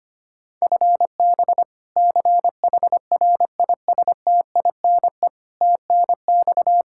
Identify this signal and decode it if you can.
Morse code, heard as a keyed tone, about 25 wpm, 700 Hz. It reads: FB CHRISTINE TNX
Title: Practice QSO - Exchange 3 - More Info - 25wpm